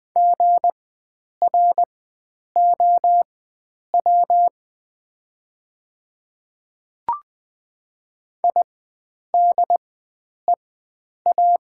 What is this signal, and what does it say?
Morse code, heard as a keyed tone, about 20 wpm, spaced as for 9 wpm, 690 Hz. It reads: GROW IDEA
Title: 4x Character Spacing - Top 500 Words - 20wpm